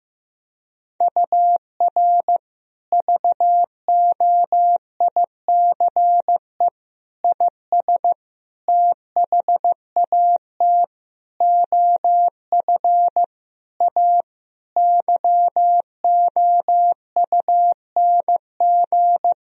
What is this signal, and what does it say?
Morse code, heard as a keyed tone, about 15 wpm, 700 Hz. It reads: UR VOICE IS THAT OF A YOUNG
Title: Sentences from Top 500 Words and Abbreviations 15wpm